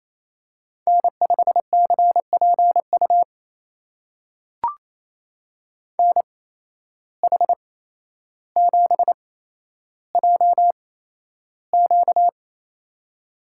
Morse code, heard as a keyed tone, about 28 wpm, spaced as for 12 wpm, 700 Hz. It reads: N5CPU NH7JQ
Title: USA Call Signs - 8x Character Spacing - 28wpm